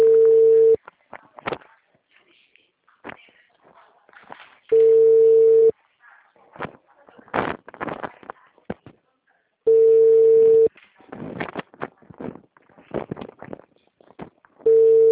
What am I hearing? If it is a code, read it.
Ringback tone (China, 450 Hz), with speech.